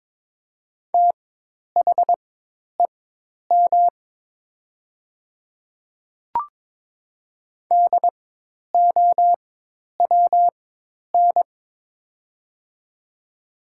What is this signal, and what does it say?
Morse code, heard as a keyed tone, about 23 wpm, spaced as for 10 wpm, 700 Hz. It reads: THEM DOWN
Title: 4x Character Spacing - Top 100 Words - 22wpm